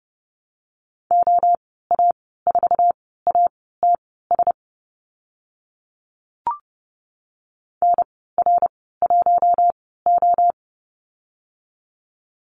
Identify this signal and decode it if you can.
Morse code, heard as a keyed tone, about 30 wpm, spaced as for 17 wpm, 700 Hz. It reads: OA4ATS NR1O
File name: Lesson 14g - Number 2 - Call Signs 30wpm